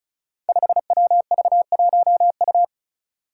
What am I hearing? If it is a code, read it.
Morse code, heard as a keyed tone, about 34 wpm, 700 Hz. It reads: 5WV1U
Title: Five Challenging Letter-Number Combinations 35wpm